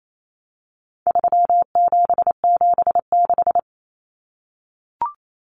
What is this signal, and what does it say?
Morse code, heard as a keyed tone, about 28 wpm, 700 Hz. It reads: 3776
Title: Numbers - 4-Digit 28wpm